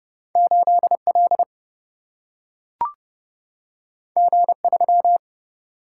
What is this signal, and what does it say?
Morse code, heard as a keyed tone, about 30 wpm, 700 Hz. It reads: 8L G3
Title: Single-Letter Single-Number 30wpm